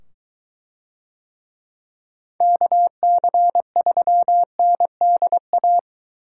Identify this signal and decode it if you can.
Morse code, heard as a keyed tone, about 23 wpm, 700 Hz. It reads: KC3NDA